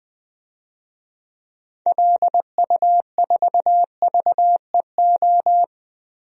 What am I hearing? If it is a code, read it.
Morse code, heard as a keyed tone, about 21 wpm, 700 Hz. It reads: LU4VEO
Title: DXCC-100 Call Signs - 6 Characters - Repeated 2x - 20wpm